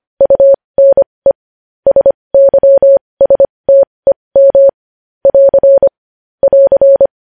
Morse code, heard as a keyed tone, about 25 wpm, spaced as for 20 wpm, 550 Hz. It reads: UNE SYSTEM <AR> <AR>
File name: podcast / News headlines-1x-25x20